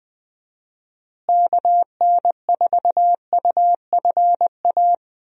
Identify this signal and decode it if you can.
Morse code, heard as a keyed tone, about 21 wpm, 700 Hz. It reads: KN4UFA
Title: USA Call Signs - 6 Characters - Repeated 3x - 20wpm